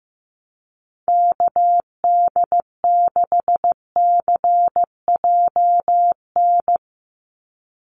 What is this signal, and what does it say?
Morse code, heard as a keyed tone, about 15 wpm, 700 Hz. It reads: KD6CJN
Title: USA Call Signs 15wpm